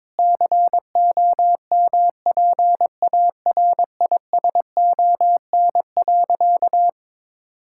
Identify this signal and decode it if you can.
Morse code, heard as a keyed tone, about 22 wpm, 700 Hz. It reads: COMPARISON.
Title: Sentences from Top 1600 Words 22wpm